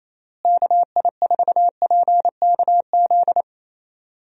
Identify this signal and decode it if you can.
Morse code, heard as a keyed tone, about 28 wpm, 700 Hz. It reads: KI4PKZ
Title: USA Call Signs - 6 Characters - Repeated 3x - 28wpm